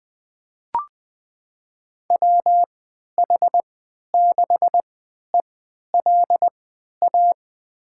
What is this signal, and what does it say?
Morse code, heard as a keyed tone, about 21 wpm, spaced as for 12 wpm, 700 Hz. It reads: WH6ELA
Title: USA Call Signs - 6 Characters - 3x Character Spacing - 20wpm